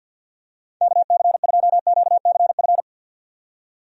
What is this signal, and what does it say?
Morse code, heard as a keyed tone, about 51 wpm, 700 Hz. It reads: KC2QCF
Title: USA Call Signs - 6 Characters - 50wpm